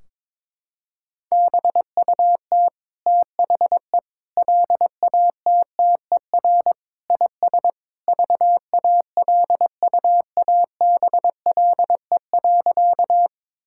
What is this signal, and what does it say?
Morse code, heard as a keyed tone, about 22 wpm, 700 Hz. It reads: BUT THE LATTER IS VALUABLE.
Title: Sentences from Top 1600 Words 22wpm